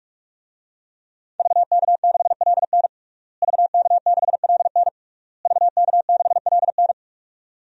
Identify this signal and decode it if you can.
Morse code, heard as a keyed tone, about 46 wpm, 700 Hz. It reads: VK6LN VK6LN VK6LN
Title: DXCC-100 Call Signs - 5 Characters - Repeated 3x - 45wpm